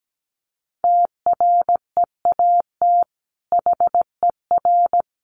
Morse code, heard as a keyed tone, about 17 wpm, 700 Hz. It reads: TREAT HER
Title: Sentences from Top 1300 Words 17wpm